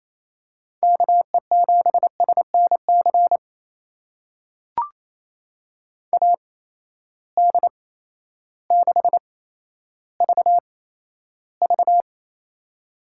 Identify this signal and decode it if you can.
Morse code, heard as a keyed tone, about 28 wpm, spaced as for 13 wpm, 700 Hz. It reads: KE7SNC AD6VV
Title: USA Call Signs - 8x Character Spacing - 28wpm